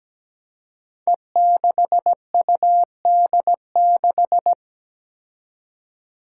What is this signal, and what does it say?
Morse code, heard as a keyed tone, about 17 wpm, 700 Hz. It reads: E6UD6
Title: Five Challenging Letter-Number Combinations 17wpm